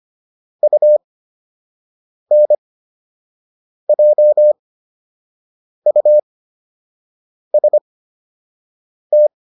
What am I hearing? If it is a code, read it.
Morse code, heard as a keyed tone, about 25 wpm, spaced as for 6 wpm, 600 Hz. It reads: UNJUST